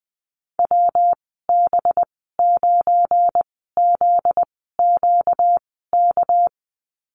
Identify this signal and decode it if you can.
Morse code, heard as a keyed tone, about 20 wpm, spaced as for 14 wpm, 700 Hz. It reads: WB9ZQK
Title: 245 Lesson 39g - Letter z - Call Signs 20wpm